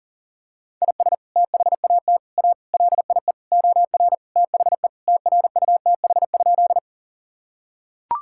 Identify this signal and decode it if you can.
Morse code, heard as a keyed tone, about 37 wpm, 700 Hz. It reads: IS THAT A LIE OR THE TRUTH?
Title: Sentences from Top 1600 Words 40wpm